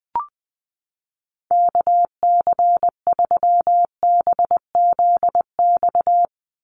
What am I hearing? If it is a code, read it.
Morse code, heard as a keyed tone, about 20 wpm, 700 Hz. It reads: KC3BZX
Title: USA Call Signs 20wpm